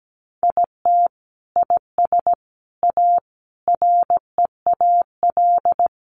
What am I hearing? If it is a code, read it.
Morse code, heard as a keyed tone, about 17 wpm, 700 Hz. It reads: IT IS A REAL
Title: Sentences from Top 1400 Words 17wpm